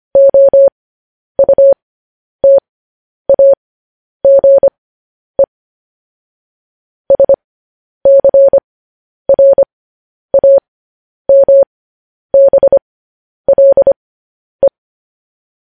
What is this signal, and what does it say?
Morse code, heard as a keyed tone, about 25 wpm, spaced as for 10 wpm, 550 Hz. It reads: OUTAGE SCRAMBLE
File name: podcast / News headlines-1x-25x10